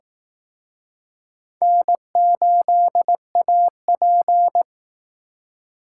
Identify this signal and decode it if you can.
Morse code, heard as a keyed tone, about 18 wpm, 700 Hz. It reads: N8AP
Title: Speed Racing - USA 1x2 Call Signs 15wpm